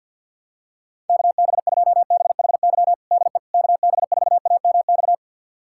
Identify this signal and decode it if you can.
Morse code, heard as a keyed tone, about 47 wpm, 700 Hz. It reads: KB2BHY DE KB4AMX
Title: USA Callsign DE Callsign - 6 Characters - 50wpm